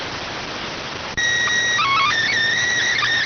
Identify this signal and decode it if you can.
SSTV (slow-scan television), PD 290.